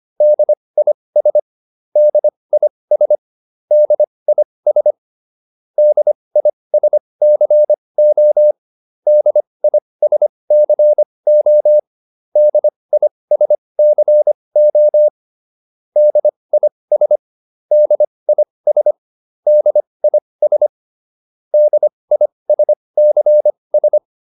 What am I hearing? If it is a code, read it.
Morse code, heard as a keyed tone, about 25 wpm, spaced as for 20 wpm, 600 Hz. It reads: DIS DIS DIS DISCO DISCO DISCO DIS DIS DIS DISCS